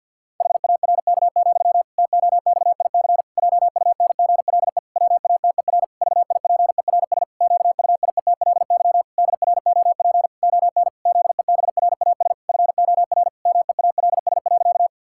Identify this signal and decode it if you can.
Morse code, heard as a keyed tone, about 49 wpm, 700 Hz. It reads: HARK, TOXIC JUNGLE WATER VIPERS QUIETLY DROP ON ZEBRAS FOR MEALS.
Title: Pangram Challenge v2 - 50wpm